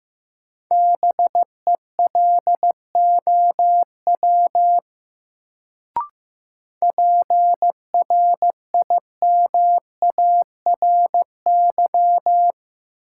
Morse code, heard as a keyed tone, about 15 wpm, 700 Hz. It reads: BELOW PRIMARY
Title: Sentences from Top 1000 Words - Review Part 5 - New Word Drill - 15wpm